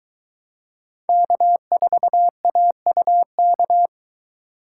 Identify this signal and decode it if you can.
Morse code, heard as a keyed tone, about 21 wpm, 700 Hz. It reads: K4AUK